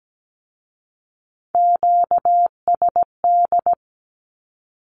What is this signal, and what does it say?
Morse code, heard as a keyed tone, about 17 wpm, 700 Hz. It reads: QSD
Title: Q-Codes with Spoken Meaning - 17wpm